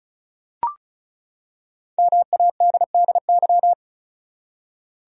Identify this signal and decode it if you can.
Morse code, heard as a keyed tone, about 36 wpm, 700 Hz. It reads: MADDY
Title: USA Common Names - Max 5 Letters - 35wpm